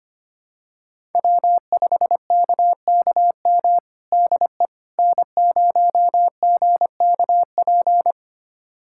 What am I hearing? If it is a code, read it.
Morse code, heard as a keyed tone, about 25 wpm, 700 Hz. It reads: W5KKM DE N0GKP
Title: USA Callsign DE Callsign - 5 Characters - 25wpm